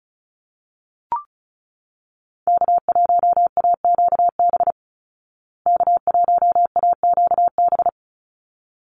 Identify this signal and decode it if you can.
Morse code, heard as a keyed tone, about 34 wpm, 700 Hz. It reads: K1AQB K1AQB
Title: Call Signs Repeated 3x - 35wpm